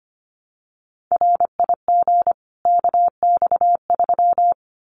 Morse code, heard as a keyed tone, about 25 wpm, 700 Hz. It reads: RIG KX3
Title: Common QSO Phrases 25wpm